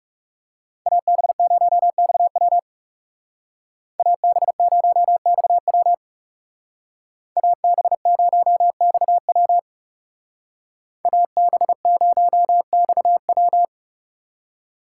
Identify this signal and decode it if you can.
Morse code, heard as a keyed tone, about 39 wpm, 700 Hz. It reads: AB0XW AB0XW AB0XW AB0XW